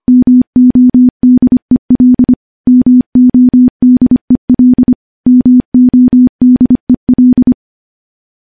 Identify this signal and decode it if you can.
Morse code, heard as a keyed tone, about 25 wpm, 250 Hz. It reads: MODEL MODEL MODEL
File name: morse / morse-training-25WPM